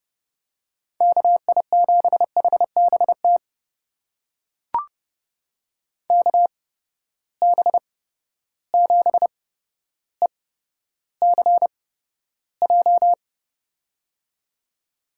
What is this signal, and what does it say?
Morse code, heard as a keyed tone, about 30 wpm, spaced as for 13 wpm, 700 Hz. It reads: KI7HBT KB7ECJ
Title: USA Call Signs - 6 Characters - 8x Character Spacing - 30wpm